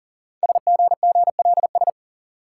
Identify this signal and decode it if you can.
Morse code, heard as a keyed tone, about 40 wpm, 690 Hz. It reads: SGGLS
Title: Code Groups - 5 Letters - 40wpm